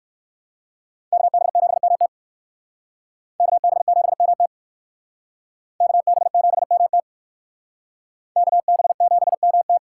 Morse code, heard as a keyed tone, about 65 wpm, 700 Hz. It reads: KB7MT KB7MT KB7MT KB7MT